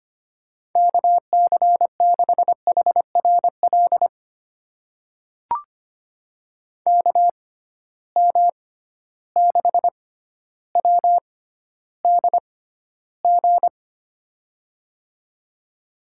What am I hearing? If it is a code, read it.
Morse code, heard as a keyed tone, about 25 wpm, spaced as for 13 wpm, 700 Hz. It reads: KC6HRL KM6WDG
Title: USA Call Signs - 6 Characters - 6x Character Spacing - 25wpm